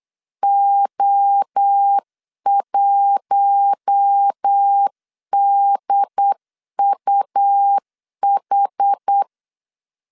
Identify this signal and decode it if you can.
Morse code, heard as a keyed tone, about 8 wpm, 800 Hz. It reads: O1DUH
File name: CW / DET / 1 6 1500C S C 8wpm 5219